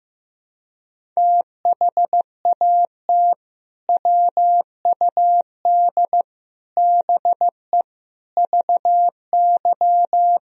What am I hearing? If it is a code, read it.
Morse code, heard as a keyed tone, about 15 wpm, 700 Hz. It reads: THAT WUD BE VY
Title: Sentences from Top 200 Words and Abbreviations 15wpm